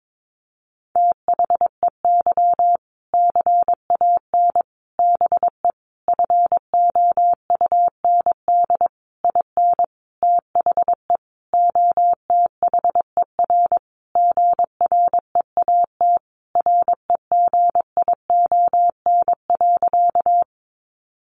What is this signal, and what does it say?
Morse code, heard as a keyed tone, about 22 wpm, 700 Hz. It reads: THEY CAN BE FOUND IN THE OTHER GREAT REGION.
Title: Sentences from Top 1500 Words 22wpm